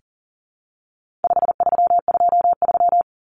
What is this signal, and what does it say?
Morse code, heard as a keyed tone, about 40 wpm, 700 Hz. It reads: 5323
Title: Numbers - 4-Digit 40wpm